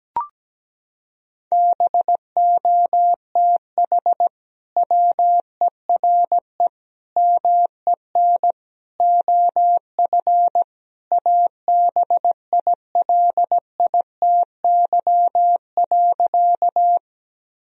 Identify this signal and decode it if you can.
Morse code, heard as a keyed tone, about 17 wpm, 700 Hz. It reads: BOTH WERE MEN OF ABILITY.
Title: Sentences from Top 1000 Words - Review Part 3 - 17wpm